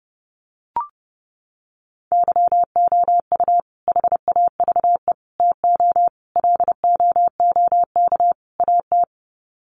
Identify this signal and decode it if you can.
Morse code, heard as a keyed tone, about 30 wpm, 700 Hz. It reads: YOU HAVE TO LOOK AT
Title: Sets of 5 Words Encore 30wpm